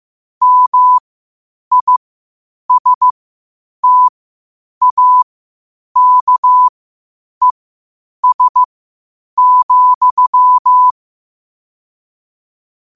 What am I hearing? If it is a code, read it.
Morse code, heard as a keyed tone, about 15 wpm, spaced as for 8 wpm, 1000 Hz. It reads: MISTAKES,
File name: qotdfiles / 8wpm